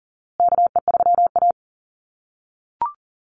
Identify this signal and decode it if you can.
Morse code, heard as a keyed tone, about 39 wpm, 700 Hz. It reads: KE3A
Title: USA Call Signs Repeated 2x 40wpm